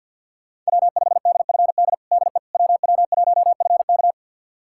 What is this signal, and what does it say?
Morse code, heard as a keyed tone, about 52 wpm, 700 Hz. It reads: W5DFD DE WW1FK
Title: USA Callsign DE Callsign - 5 Characters - 50wpm